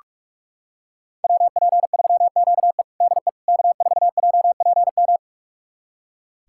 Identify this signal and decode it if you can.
Morse code, heard as a keyed tone, about 42 wpm, 700 Hz. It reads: WP3QE DE K4JPM
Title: USA Callsign DE Callsign - 5 Characters - 45wpm